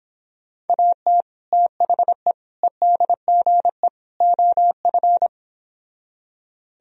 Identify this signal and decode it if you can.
Morse code, heard as a keyed tone, about 29 wpm, 700 Hz. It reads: AT THE EDGE OF